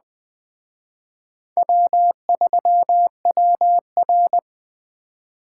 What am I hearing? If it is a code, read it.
Morse code, heard as a keyed tone, about 21 wpm, 700 Hz. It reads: W3WR
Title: Speed Racing - USA 1x2 Call Signs 17wpm